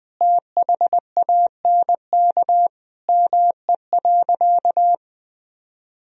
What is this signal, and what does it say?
Morse code, heard as a keyed tone, about 19 wpm, 700 Hz. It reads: THANK ME.
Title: Sentences from Top 1000 Words - Review Part 1 - 20wpm